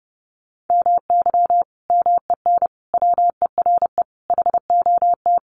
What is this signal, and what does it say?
Morse code, heard as a keyed tone, about 30 wpm, 700 Hz. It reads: MY MEN WERE HOT
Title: Sentences from Top 1100 Words 30wpm